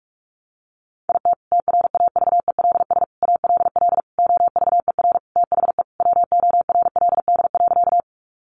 Morse code, heard as a keyed tone, about 44 wpm, 700 Hz. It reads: IT TRAVELS ALL OVER THE WORLD.
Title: Additional Easy Phrases 45wpm